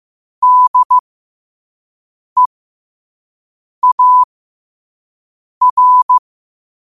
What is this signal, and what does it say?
Morse code, heard as a keyed tone, about 15 wpm, spaced as for 5 wpm, 1000 Hz. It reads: DEAR
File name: qotdfiles / 5wpm